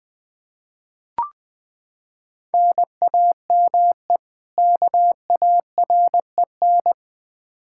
Morse code, heard as a keyed tone, about 20 wpm, 700 Hz. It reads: NAME KAREN
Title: Practice QSO - Exchange 1 - Beginning - 20wpm